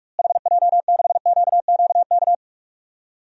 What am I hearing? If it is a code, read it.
Morse code, heard as a keyed tone, about 45 wpm, 700 Hz. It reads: HJ6QQK
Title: DXCC-100 Call Signs - 6 Characters - Repeated 3x - 45wpm